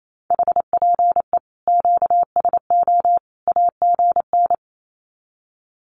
Morse code, heard as a keyed tone, about 28 wpm, 700 Hz. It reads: HPE QSO AGN
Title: Common QSO Phrases 28wpm